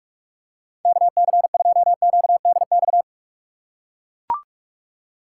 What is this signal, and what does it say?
Morse code, heard as a keyed tone, about 42 wpm, 690 Hz. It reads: KC2QDX
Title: USA Call Signs - 8x Character Spacing - 45wpm